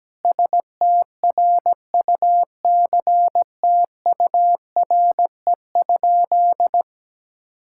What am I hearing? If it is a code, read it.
Morse code, heard as a keyed tone, about 17 wpm, 700 Hz. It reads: STRUCTURE?
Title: Sentences from Top 1000 Words - Review Part 6 - 17wpm